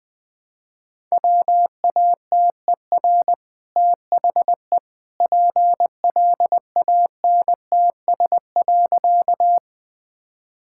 Morse code, heard as a keyed tone, about 21 wpm, 700 Hz. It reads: WATER THE PLANTS.
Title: Easy Sentences with Hints - 20wpm